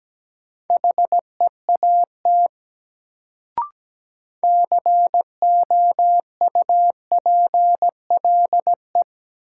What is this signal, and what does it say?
Morse code, heard as a keyed tone, about 17 wpm, 700 Hz. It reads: HEAT COUPLE
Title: Sentences from Top 1100 Words - New Word Drill Alt - 17wpm